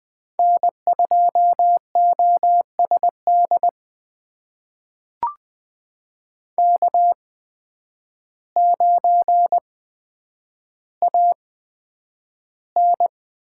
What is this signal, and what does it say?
Morse code, heard as a keyed tone, about 20 wpm, spaced as for 9 wpm, 700 Hz. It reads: N2OSD K9AN
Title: USA Call Signs - 5 Characters - 8x Character Spacing - 20wpm